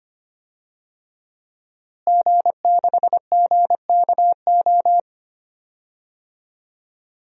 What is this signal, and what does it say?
Morse code, heard as a keyed tone, about 25 wpm, 700 Hz. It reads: G6GKO
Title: DXCC-100 Call Signs - 5 Characters - Repeated 3x - 25wpm